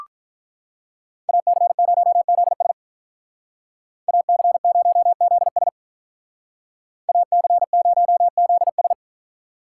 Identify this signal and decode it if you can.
Morse code, heard as a keyed tone, about 44 wpm, 700 Hz. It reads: AC0ZS AC0ZS AC0ZS